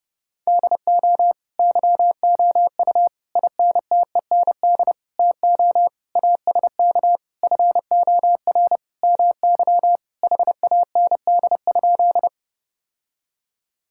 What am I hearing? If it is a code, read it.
Morse code, heard as a keyed tone, about 30 wpm, 700 Hz. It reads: DO YOU INTEND TO ASK FOR MY HAND?
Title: Sentences from Top 1600 Words 30wpm